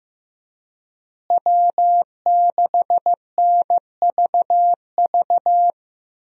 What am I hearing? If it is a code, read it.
Morse code, heard as a keyed tone, about 15 wpm, 700 Hz. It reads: W6NVV
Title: USA Call Signs - 5 Characters - Repeated 3x - 15wpm